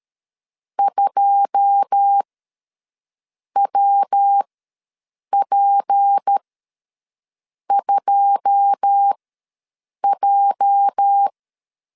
Morse code, heard as a keyed tone, about 13 wpm, spaced as for 6 wpm, 800 Hz. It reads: 2WP2J
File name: LEZIONI / 9 2 ws 12wpm 3406